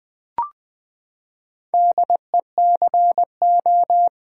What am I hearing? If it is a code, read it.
Morse code, heard as a keyed tone, about 20 wpm, 700 Hz. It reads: DECO
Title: Build A Word - 4 Letters At A Time - 8-Letter Words - 20wpm